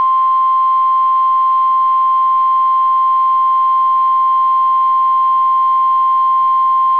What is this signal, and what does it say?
Emergency Alert System (SAME) tones (the 1050 Hz attention tone). It is heard from 0.8 to 5.9 s.